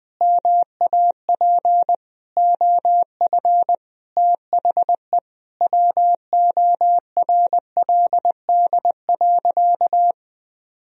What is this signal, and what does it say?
Morse code, heard as a keyed tone, about 20 wpm, 700 Hz. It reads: MAP OF THE WORLD.
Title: Sentences from Top 1000 Words - Review Part 4 - 20wpm